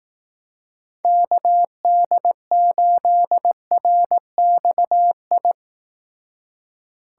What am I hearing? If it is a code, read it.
Morse code, heard as a keyed tone, about 18 wpm, 700 Hz. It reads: KD8RXI
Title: Speed Racing - USA 2x3 Call Signs 15wpm